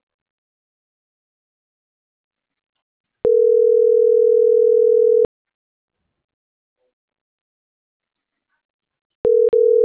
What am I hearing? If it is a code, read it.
Ringback tone (North America), 440 + 480 Hz.